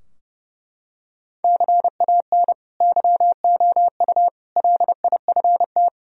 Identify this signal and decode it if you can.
Morse code, heard as a keyed tone, about 32 wpm, 700 Hz. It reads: CAN YOU LIFT
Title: Sentences from Top 1000 Words - Review Part 3 - 30wpm